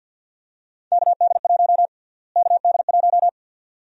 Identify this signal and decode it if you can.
Morse code, heard as a keyed tone, about 52 wpm, 700 Hz. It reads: KD1 KD1
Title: USA Call Sign Prefixes - Repeated 2x - 50wpm